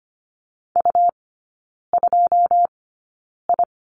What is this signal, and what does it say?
Morse code, heard as a keyed tone, about 25 wpm, spaced as for 9 wpm, 700 Hz. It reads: U2I
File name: Lesson 17e - Letter u - Three Characters 25wpm